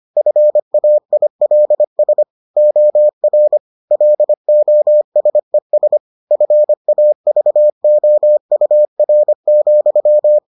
Morse code, heard as a keyed tone, about 25 wpm, 600 Hz. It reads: FAILS OR LOSES FAVOUR,